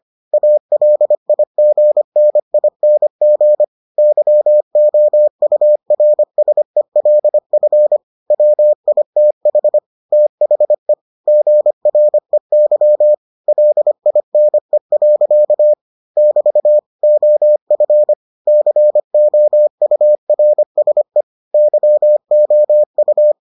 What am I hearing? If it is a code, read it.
Morse code, heard as a keyed tone, about 25 wpm, 600 Hz. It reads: ALIGNING YOURSELF WITH THE GREY LINE. <BT> OF COURSE YOU